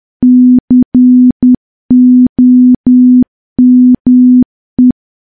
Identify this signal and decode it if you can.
Morse code, heard as a keyed tone, about 10 wpm, 250 Hz. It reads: COME
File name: morse / morse-training-10WPM